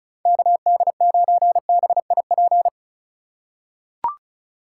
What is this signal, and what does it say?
Morse code, heard as a keyed tone, about 36 wpm, 700 Hz. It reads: KD9BIP
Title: USA Call Signs - 6 Characters - 4x Character Spacing - 35wpm